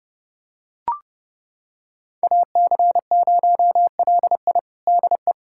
Morse code, heard as a keyed tone, about 30 wpm, 700 Hz. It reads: AC0LI DE
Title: USA Callsign DE Callsign 30wpm